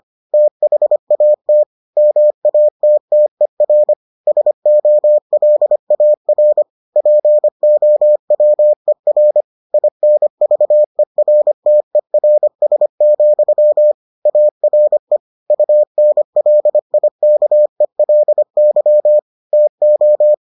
Morse code, heard as a keyed tone, about 25 wpm, 600 Hz. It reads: THAT MATTER SOLAR POWER INVERTERS, ARE UNLIKELY TO